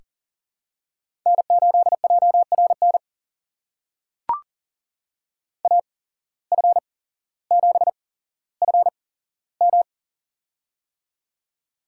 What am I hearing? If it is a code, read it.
Morse code, heard as a keyed tone, about 40 wpm, spaced as for 17 wpm, 700 Hz. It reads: N8JRN AF7FM
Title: USA Call Signs - 5 Characters - 8x Character Spacing - 40wpm